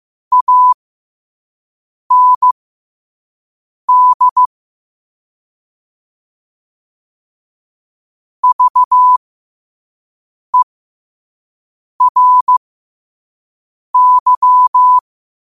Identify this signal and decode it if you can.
Morse code, heard as a keyed tone, about 15 wpm, spaced as for 5 wpm, 1000 Hz. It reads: AND VERY